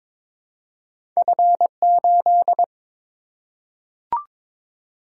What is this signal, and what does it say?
Morse code, heard as a keyed tone, about 23 wpm, 700 Hz. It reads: F8
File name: practice / 22wpm / Two Letter-Number Combinations - 22wpm